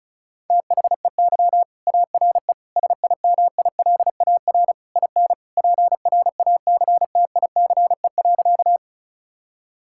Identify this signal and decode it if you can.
Morse code, heard as a keyed tone, about 36 wpm, 700 Hz. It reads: THEY ARE SIMILAR IN PRACTICE.
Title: Sentences from Top 1000 Words - Review Part 5 - 35wpm